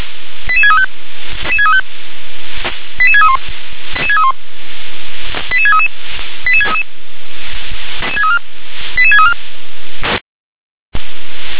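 Five-tone selective call (ZVEI), heard as quick runs of five tones, tones of about 70 ms. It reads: 88635 88631 88633 88633 88634
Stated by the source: ZVEI five-tone selective call received on an SDR.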